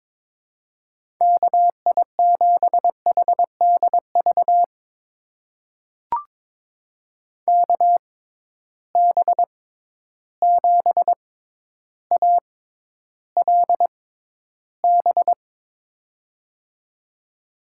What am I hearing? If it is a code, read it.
Morse code, heard as a keyed tone, about 22 wpm, spaced as for 11 wpm, 700 Hz. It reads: KI7HDV KB7ALB
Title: USA Call Signs - 6x Character Spacing - 22wpm